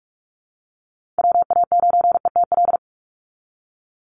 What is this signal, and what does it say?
Morse code, heard as a keyed tone, about 44 wpm, 700 Hz. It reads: WA9ETL
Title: USA Call Signs 45wpm